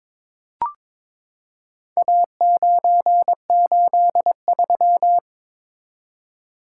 Morse code, heard as a keyed tone, about 22 wpm, 700 Hz. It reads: A983
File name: practice / 22wpm / Four Letter-Number Combinations - 22wpm